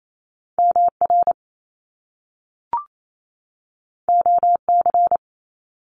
Morse code, heard as a keyed tone, about 28 wpm, 700 Hz. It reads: MR OC
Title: Top Two Letter Combinations in Words 28wpm